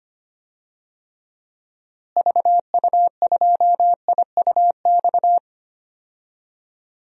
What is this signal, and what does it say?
Morse code, heard as a keyed tone, about 25 wpm, 700 Hz. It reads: VU2IUX
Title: DXCC-100 Call Signs - 6 Characters - Repeated 2x - 25wpm